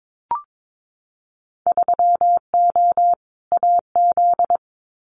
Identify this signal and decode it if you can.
Morse code, heard as a keyed tone, about 22 wpm, 700 Hz. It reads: 3O AZ
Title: Winter Field Day Exchanges 22wpm